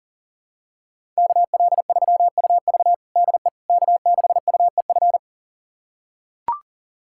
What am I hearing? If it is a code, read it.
Morse code, heard as a keyed tone, about 40 wpm, 700 Hz. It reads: KL3UV DE K6UEF
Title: USA Callsign DE Callsign - 5 Characters - 40wpm